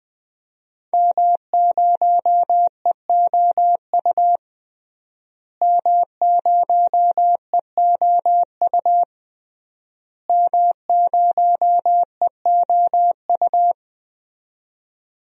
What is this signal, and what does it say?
Morse code, heard as a keyed tone, about 20 wpm, 700 Hz. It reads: M0EOU M0EOU M0EOU
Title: DXCC-100 Call Signs - 5 Characters - Repeated 3x - 20wpm